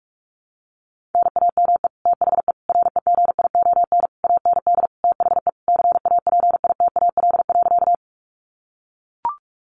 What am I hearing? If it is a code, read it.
Morse code, heard as a keyed tone, about 44 wpm, 700 Hz. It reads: NAME THE REGION AND THE CAPITAL.
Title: Sentences from Top 1500 Words 45wpm